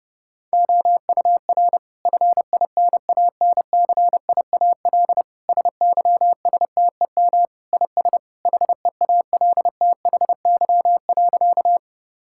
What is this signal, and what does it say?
Morse code, heard as a keyed tone, about 30 wpm, 700 Hz. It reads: OUR FINANCIAL SYSTEM IS HEALTHY.
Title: Sentences from Top 1000 Words - Review Part 5 - 30wpm